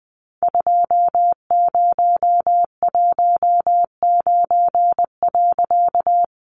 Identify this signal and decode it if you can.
Morse code, heard as a keyed tone, about 20 wpm, 700 Hz. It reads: 2019.
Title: A Sound Of Thunder 20wpm